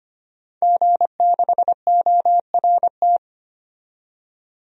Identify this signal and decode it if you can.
Morse code, heard as a keyed tone, about 25 wpm, 700 Hz. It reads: G6ORT
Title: DXCC-100 Call Signs - 5 Characters - 25wpm